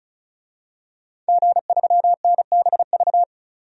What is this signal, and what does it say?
Morse code, heard as a keyed tone, about 36 wpm, 700 Hz. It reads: G3NBV
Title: DXCC-100 Call Signs - 5 Characters - 35wpm